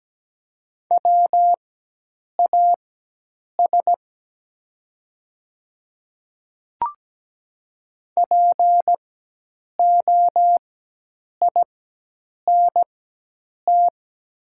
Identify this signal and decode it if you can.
Morse code, heard as a keyed tone, about 17 wpm, spaced as for 8 wpm, 700 Hz. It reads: WAS POINT
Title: 4x Character Spacing - Top 200 Words - 17wpm